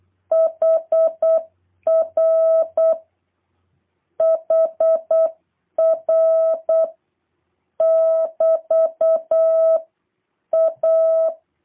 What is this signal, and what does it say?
Morse code, heard as a keyed tone, about 8 wpm, 650 Hz. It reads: HR HR <BT> A